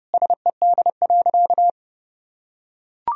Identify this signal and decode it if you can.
Morse code, heard as a keyed tone, about 30 wpm, 700 Hz. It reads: SED.
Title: Sentences from Top 200 Words and Abbreviations 30wpm